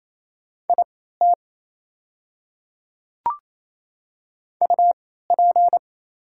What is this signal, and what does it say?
Morse code, heard as a keyed tone, about 27 wpm, spaced as for 16 wpm, 690 Hz. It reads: IT UP
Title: 3x Character Spacing - Top 100 Words - 28wpm